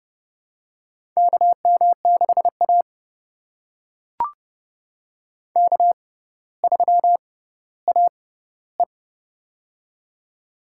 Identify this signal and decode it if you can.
Morse code, heard as a keyed tone, about 31 wpm, spaced as for 16 wpm, 700 Hz. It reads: KM6A K3AE
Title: USA Call Signs - 6x Character Spacing - 30wpm